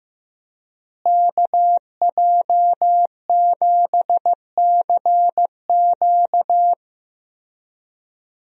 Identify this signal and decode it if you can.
Morse code, heard as a keyed tone, about 16 wpm, 700 Hz. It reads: KJ7CQ
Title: USA Call Signs - 5 Characters - Repeated 3x - 15wpm